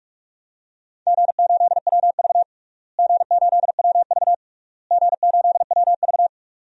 Morse code, heard as a keyed tone, about 41 wpm, 700 Hz. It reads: G8WV G8WV G8WV
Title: DXCC-100 Call Signs - 4 Characters - Repeated 3x - 45wpm